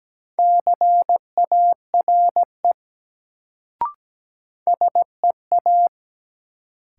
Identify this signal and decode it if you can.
Morse code, heard as a keyed tone, about 17 wpm, 700 Hz. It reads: CARE SEA
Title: Sentences from Top 300 Words - New Word Drill - 17wpm